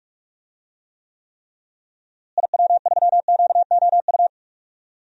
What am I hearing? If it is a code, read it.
Morse code, heard as a keyed tone, about 41 wpm, 700 Hz. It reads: IW3QOU
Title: DXCC-100 Call Signs - 6 Characters - Repeated 3x - 45wpm